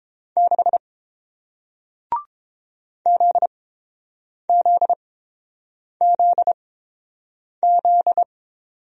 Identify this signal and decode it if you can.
Morse code, heard as a keyed tone, about 30 wpm, spaced as for 7 wpm, 700 Hz. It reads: 6 ZZZZ